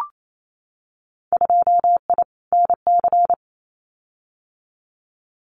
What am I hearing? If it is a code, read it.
Morse code, heard as a keyed tone, about 28 wpm, 700 Hz. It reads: 2I NC
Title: Winter Field Day Exchanges 28wpm